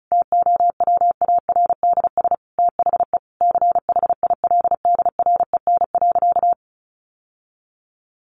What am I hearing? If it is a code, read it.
Morse code, heard as a keyed tone, about 34 wpm, 700 Hz. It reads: TOWARDS THE CHILDREN.
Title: Sentences from Top 1100 Words 35wpm